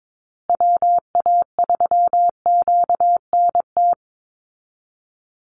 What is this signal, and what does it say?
Morse code, heard as a keyed tone, about 22 wpm, 700 Hz. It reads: WA3QNT
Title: Call Signs Repeated 3x - 22wpm